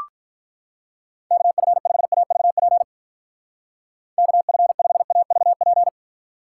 Morse code, heard as a keyed tone, about 45 wpm, 700 Hz. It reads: KF5AVP KF5AVP